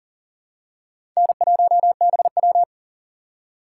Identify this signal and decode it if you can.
Morse code, heard as a keyed tone, about 40 wpm, 700 Hz. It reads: N1BW
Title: USA Call Signs - 2x Character Spacing - 40wpm